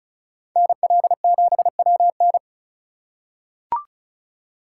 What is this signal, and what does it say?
Morse code, heard as a keyed tone, about 34 wpm, 700 Hz. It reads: NL7WN
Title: USA Call Signs - 5 Characters - 4x Character Spacing - 35wpm